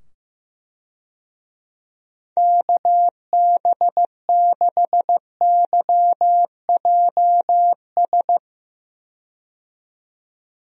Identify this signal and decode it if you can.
Morse code, heard as a keyed tone, about 15 wpm, 700 Hz. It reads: KB6YJS
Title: USA Call Signs - 6 Characters - 15wpm